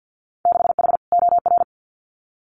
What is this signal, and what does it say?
Morse code, heard as a keyed tone, about 50 wpm, 700 Hz. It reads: 6H OR
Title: Winter Field Day Exchanges Encore 50wpm